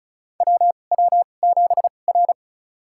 Morse code, heard as a keyed tone, about 36 wpm, spaced as for 25 wpm, 700 Hz. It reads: WW7R
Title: USA Call Signs - 4 Characters - 2x Character Spacing - 35wpm